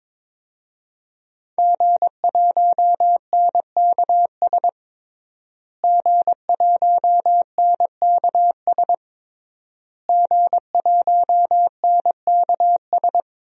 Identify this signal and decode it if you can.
Morse code, heard as a keyed tone, about 22 wpm, 700 Hz. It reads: G1NKS G1NKS G1NKS
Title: DXCC-100 Call Signs - 5 Characters - Repeated 3x - 22wpm